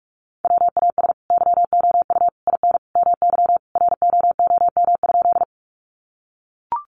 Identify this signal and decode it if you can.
Morse code, heard as a keyed tone, about 44 wpm, 700 Hz. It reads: WAS YOU IN MY ROOM?
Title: Sentences from Top 300 Words 45wpm